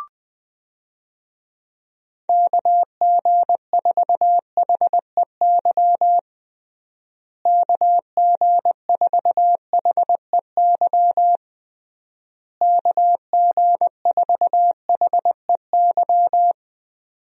Morse code, heard as a keyed tone, about 20 wpm, 700 Hz. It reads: KG4HEY KG4HEY KG4HEY
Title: USA Call Signs - 6 Characters - Repeated 3x - 20wpm